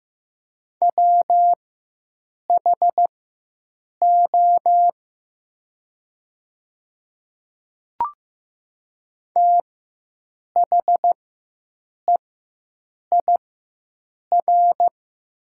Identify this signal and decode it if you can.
Morse code, heard as a keyed tone, about 15 wpm, spaced as for 7 wpm, 700 Hz. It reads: WHO THEIR
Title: 4x Character Spacing - Top 100 Words - 15wpm